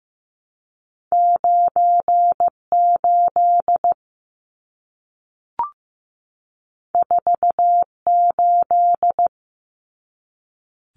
Morse code, heard as a keyed tone, about 15 wpm, 700 Hz. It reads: 98 48
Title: Numbers - 2-Digit 15wpm